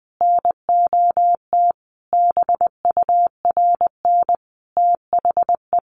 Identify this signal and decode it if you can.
Morse code, heard as a keyed tone, about 20 wpm, 700 Hz. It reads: NOT BURN THE
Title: Sentences from Top 1300 Words 20wpm